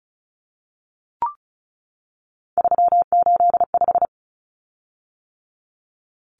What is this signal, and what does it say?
Morse code, heard as a keyed tone, about 34 wpm, 700 Hz. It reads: 385
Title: Common Cut Numbers - 3-Digit 35wpm